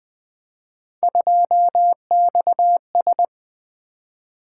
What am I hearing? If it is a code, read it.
Morse code, heard as a keyed tone, about 20 wpm, 700 Hz. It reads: 2XS
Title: Three Letter-Number Combinations - 20wpm